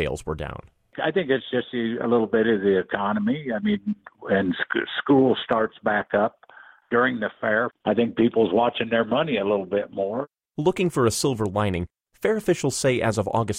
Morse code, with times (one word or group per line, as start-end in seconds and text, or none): none